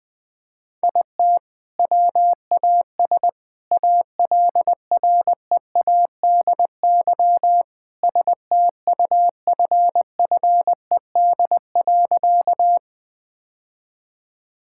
Morse code, heard as a keyed tone, about 21 wpm, 700 Hz. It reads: IT WAS ALREADY STUFFED.
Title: All-Age Appropriate Jokes - 20wpm